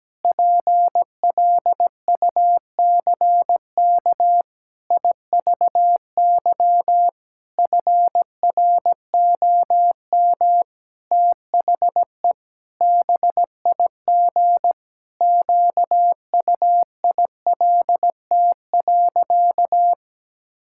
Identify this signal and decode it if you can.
Morse code, heard as a keyed tone, about 17 wpm, 700 Hz. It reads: PLUCK IVY FROM THE BIG QUILT.
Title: Pangram Challenge v2 - 17wpm